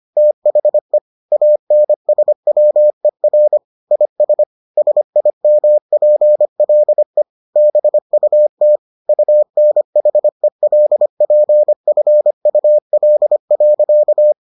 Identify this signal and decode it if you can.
Morse code, heard as a keyed tone, about 25 wpm, 600 Hz. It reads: THE ANSWER IS SIMPLE BUT UNHELPFUL.